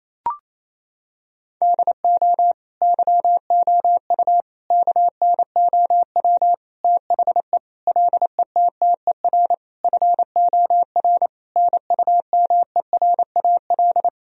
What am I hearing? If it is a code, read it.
Morse code, heard as a keyed tone, about 27 wpm, 700 Hz. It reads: DO YOU KNOW THE LETTER FOR NUMERAL
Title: Sentences from Top 1000 Words - Review Part 2 - 28wpm